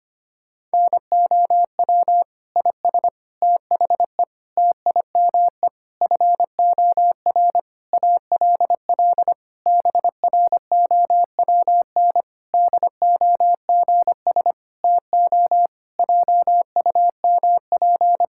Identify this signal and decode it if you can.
Morse code, heard as a keyed tone, about 25 wpm, 700 Hz. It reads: NOW IS THE TIME FOR ALL BROWN DOGS TO JUMP
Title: Pangram Challenge v2 - 25wpm